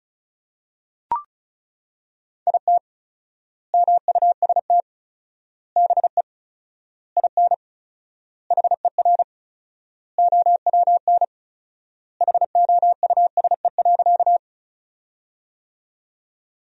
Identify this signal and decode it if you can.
Morse code, heard as a keyed tone, about 36 wpm, 700 Hz. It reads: IT MUST BE IN HER OWN HOUSE.
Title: 4x Word-Spacing - Sentences from Top 300 Words - 35wpm